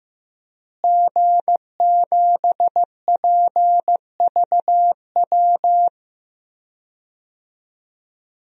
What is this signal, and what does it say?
Morse code, heard as a keyed tone, about 15 wpm, 700 Hz. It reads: G7PVW
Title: DXCC-100 Call Signs - 5 Characters - Repeated 3x - 15wpm